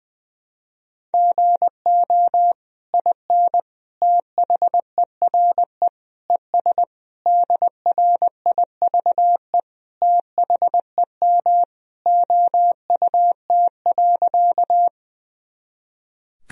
Morse code, heard as a keyed tone, about 20 wpm, 700 Hz. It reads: GO IN THERE ES DRIVE THEM OUT.
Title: Sentences from Top 500 Words and Abbreviations 20wpm